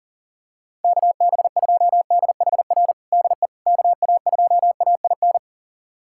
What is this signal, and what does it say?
Morse code, heard as a keyed tone, about 40 wpm, 700 Hz. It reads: KB2DHR DE KA2AIN
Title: USA Callsign DE Callsign - 6 Characters - 40wpm